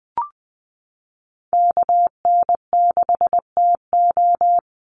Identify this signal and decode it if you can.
Morse code, heard as a keyed tone, about 20 wpm, 700 Hz. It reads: KN6TO
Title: USA Call Signs 20wpm